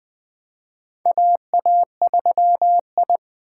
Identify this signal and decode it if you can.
Morse code, heard as a keyed tone, about 21 wpm, 700 Hz. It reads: AA3I
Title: USA Call Signs - 4 Characters - 2x Character Spacing - 20wpm